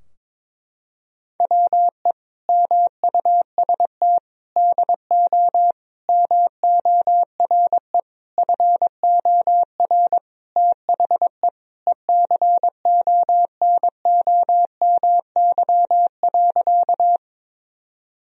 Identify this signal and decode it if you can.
Morse code, heard as a keyed tone, about 22 wpm, 700 Hz. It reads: WE MUST DO MORE FOR THE ECONOMY.
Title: Sentences from Top 1000 Words - Review Part 3 - 22wpm